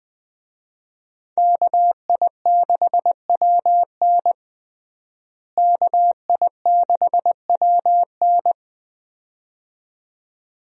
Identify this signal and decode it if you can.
Morse code, heard as a keyed tone, about 21 wpm, 700 Hz. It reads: KI6WN KI6WN
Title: USA Call Signs - 5 Characters - Repeated 2x - 20wpm